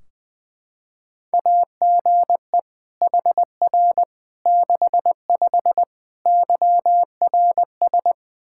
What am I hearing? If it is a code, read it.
Morse code, heard as a keyed tone, about 20 wpm, 700 Hz. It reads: AGE HR 65 YRS
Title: Practice QSO - Exchange 3 - More Info - 20wpm